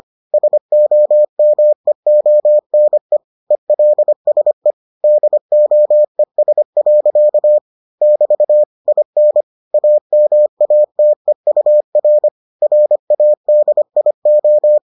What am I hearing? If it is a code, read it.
Morse code, heard as a keyed tone, about 25 wpm, 600 Hz. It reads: SOMEONE ELSE DOES. <BT> IN AMATEUR RADIO